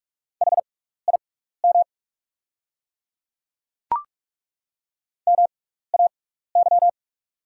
Morse code, heard as a keyed tone, about 43 wpm, spaced as for 15 wpm, 690 Hz. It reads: HIM MAY
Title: 6x Character Spacing - Top 100 Words - 45wpm